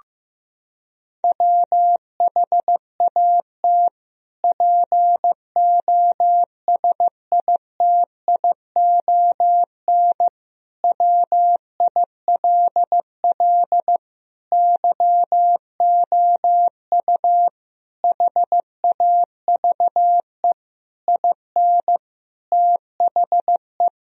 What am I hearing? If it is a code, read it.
Morse code, heard as a keyed tone, about 16 wpm, 700 Hz. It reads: WHAT POSITION WILL YOU HAVE IN THE
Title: Sentences from Top 1000 Words - Review Part 3 - 15wpm